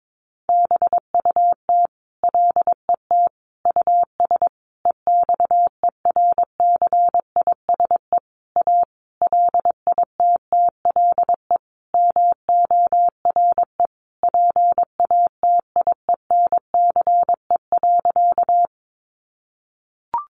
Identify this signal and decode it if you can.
Morse code, heard as a keyed tone, about 22 wpm, 700 Hz. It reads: BUT LET US EXERCISE A LITTLE MORE PATIENCE.
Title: Sentences from Top 1300 Words 22wpm